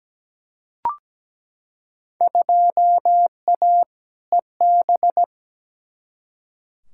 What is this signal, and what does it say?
Morse code, heard as a keyed tone, about 17 wpm, 700 Hz. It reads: 2A EB
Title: Field Day Exchanges - 17wpm